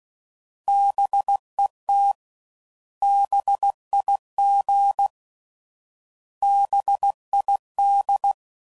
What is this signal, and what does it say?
Morse code, heard as a keyed tone, about 16 wpm, 800 Hz. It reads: BET BIG BID